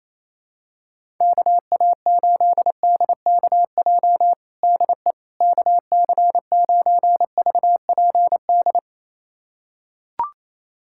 Morse code, heard as a keyed tone, about 27 wpm, 700 Hz. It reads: KA8DKJ DE KC9VPD
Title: USA Callsign DE Callsign - 6 Characters - 28wpm